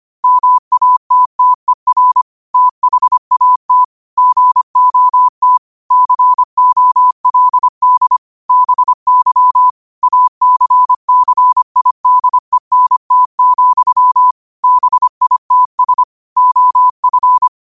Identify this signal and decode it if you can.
Morse code, heard as a keyed tone, about 25 wpm, 1000 Hz. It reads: MATTER THAT GOT COLD BY ACCIDENT, BITS OF